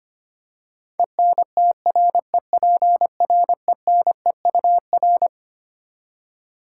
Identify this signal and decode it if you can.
Morse code, heard as a keyed tone, about 25 wpm, 700 Hz. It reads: ENTREPRENEUR
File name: Twelve Letter Words - 25wpm